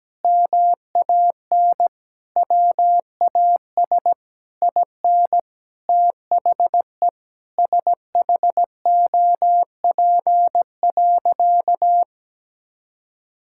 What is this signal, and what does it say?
Morse code, heard as a keyed tone, about 17 wpm, 700 Hz. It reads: MAN WAS IN THE SHOP.
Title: Sentences from Top 1600 Words 17wpm